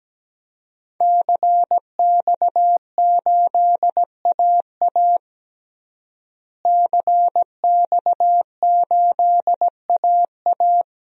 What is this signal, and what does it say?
Morse code, heard as a keyed tone, about 17 wpm, 700 Hz. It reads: CX8AA CX8AA
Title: DXCC-100 Call Signs - 5 Characters - Repeated 3x - 17wpm